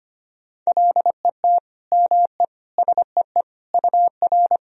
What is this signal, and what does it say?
Morse code, heard as a keyed tone, about 25 wpm, 700 Hz. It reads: LET ME SEE UR
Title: Sentences from Top 500 Words and Abbreviations 25wpm